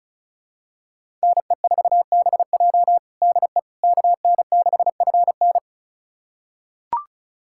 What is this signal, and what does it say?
Morse code, heard as a keyed tone, about 36 wpm, 700 Hz. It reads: NE4BJ DE KN6FN
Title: USA Callsign DE Callsign - 5 Characters - 35wpm